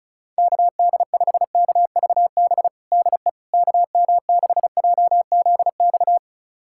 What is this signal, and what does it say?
Morse code, heard as a keyed tone, about 36 wpm, 700 Hz. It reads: KD5KVB DE KM6JZX
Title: USA Callsign DE Callsign - 6 Characters - 35wpm